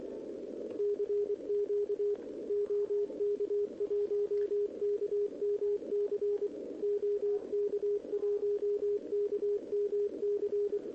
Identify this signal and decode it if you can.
Morse code, heard as a keyed tone, about 24 wpm, 410 Hz, with radio noise. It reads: CQ OK1KMC OK1KMC
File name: OK QRP 2026 / OKQRP2026